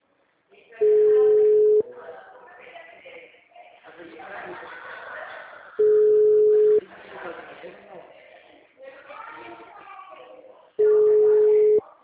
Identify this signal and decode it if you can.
Ringback tone (Europe, 425 Hz), with speech.